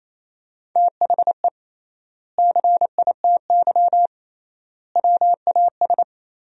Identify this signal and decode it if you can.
Morse code, heard as a keyed tone, about 28 wpm, 700 Hz. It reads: THE CITY WAS
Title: 3x Word-Spacing - Sentences from Top 500 Words - 28wpm